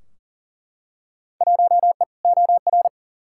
Morse code, heard as a keyed tone, about 40 wpm, 700 Hz. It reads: 1E OR
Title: Field Day Exchanges - 40wpm